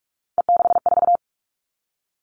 Morse code, heard as a keyed tone, about 46 wpm, 700 Hz. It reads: E64
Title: Three Challenging Letter-Number Combinations 45wpm